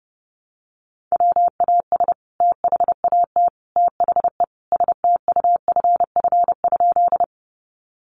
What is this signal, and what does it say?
Morse code, heard as a keyed tone, about 30 wpm, 700 Hz. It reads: WAS THAT THE STUFF?
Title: Sentences from Top 1400 Words 30wpm